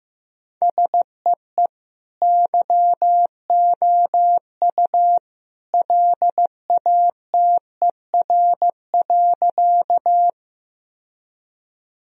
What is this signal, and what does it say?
Morse code, heard as a keyed tone, about 16 wpm, 700 Hz. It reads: SEE YOU LATER.
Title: Sentences from Top 1000 Words - Review Part 4 - 15wpm